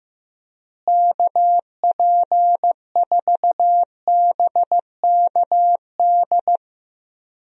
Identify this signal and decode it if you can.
Morse code, heard as a keyed tone, about 15 wpm, 700 Hz. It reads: KP4BKD
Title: DXCC-100 Call Signs - 6 Characters - Repeated 3x - 15wpm